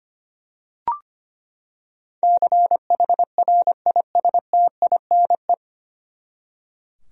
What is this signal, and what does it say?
Morse code, heard as a keyed tone, about 25 wpm, 700 Hz. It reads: CHRISTINE
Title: USA Common Names - 25wpm